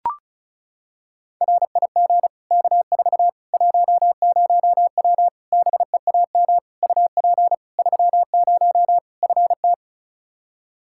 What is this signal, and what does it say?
Morse code, heard as a keyed tone, about 36 wpm, 700 Hz. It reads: RIG K4 10W BEAM UP 30 FT
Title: K1USN SST QSO Practice 35wpm